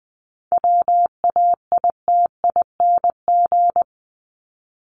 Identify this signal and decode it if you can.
Morse code, heard as a keyed tone, about 20 wpm, 700 Hz. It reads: WAITING
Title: ING Suffix - Top 100 Words with Base Words 20wpm